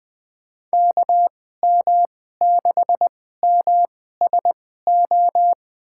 Morse code, heard as a keyed tone, about 21 wpm, spaced as for 15 wpm, 700 Hz. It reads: KM6MSO
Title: USA Call Signs - 2x Character Spacing - 20wpm